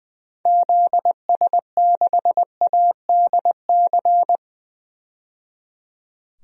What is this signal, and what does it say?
Morse code, heard as a keyed tone, about 20 wpm, 700 Hz. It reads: ZS6ADC
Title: DXCC-100 Call Signs - 6 Characters - 20wpm